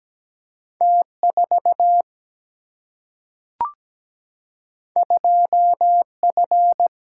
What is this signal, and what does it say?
Morse code, heard as a keyed tone, about 17 wpm, 700 Hz. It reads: T4 2F
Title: Single-Letter Single-Number 17wpm